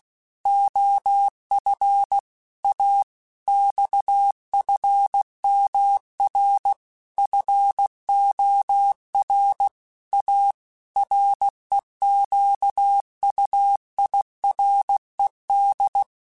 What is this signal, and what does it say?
Morse code, heard as a keyed tone, about 16 wpm, 800 Hz. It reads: OF A XFMR FOR A REQUIRED
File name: Technical Subjects / Technical Subjects-mp3s / Sw Pwr Sup @15